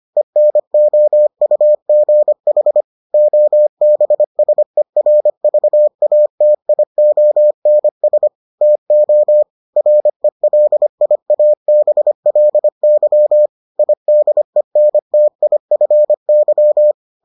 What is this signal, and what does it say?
Morse code, heard as a keyed tone, about 25 wpm, 600 Hz. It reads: ENOUGH OBSERVATIONS TO RELIABLY IDENTIFY